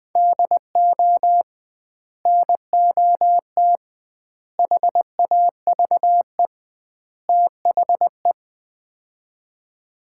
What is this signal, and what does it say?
Morse code, heard as a keyed tone, about 20 wpm, 700 Hz. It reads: DO NOT HAVE THE
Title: 2x Word-Spacing - Sets of 4 Words - 20wpm